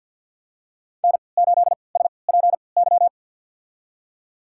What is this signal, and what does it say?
Morse code, heard as a keyed tone, about 47 wpm, spaced as for 28 wpm, 700 Hz. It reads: N8SPY
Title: USA Call Signs - 5 Characters - 3x Character Spacing - 50wpm